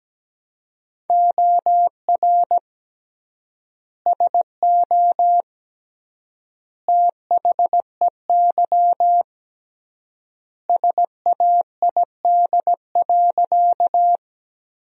Morse code, heard as a keyed tone, about 17 wpm, 700 Hz. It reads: OR SO THEY SAID.
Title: 3x Word-Spacing - Sentences from Top 100 Words - 17wpm